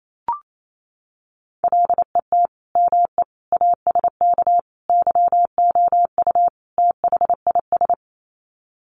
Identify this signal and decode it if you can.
Morse code, heard as a keyed tone, about 28 wpm, 700 Hz. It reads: LET ME ASK YOU THIS
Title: Sets of 5 Words 28wpm